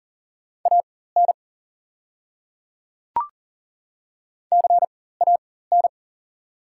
Morse code, heard as a keyed tone, about 40 wpm, spaced as for 19 wpm, 690 Hz. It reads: AN CAN
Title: 4x Character Spacing - Top 100 Words - 40wpm